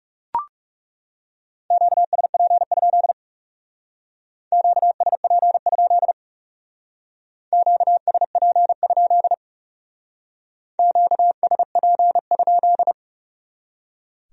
Morse code, heard as a keyed tone, about 40 wpm, 700 Hz. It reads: QSP? QSP? QSP? QSP?